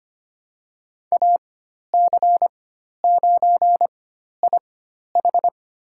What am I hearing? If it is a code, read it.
Morse code, heard as a keyed tone, about 25 wpm, spaced as for 12 wpm, 700 Hz. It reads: AC9IH
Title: USA Call Signs - 5 Characters - 4x Character Spacing - 25wpm